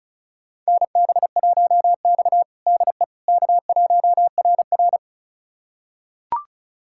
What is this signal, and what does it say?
Morse code, heard as a keyed tone, about 36 wpm, 700 Hz. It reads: NB1X DE K1RR
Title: USA Callsign DE Callsign - 4 Characters - 35wpm